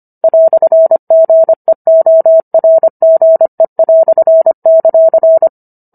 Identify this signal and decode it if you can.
Morse code, heard as a keyed tone, about 25 wpm, 650 Hz. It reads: "GEORGE";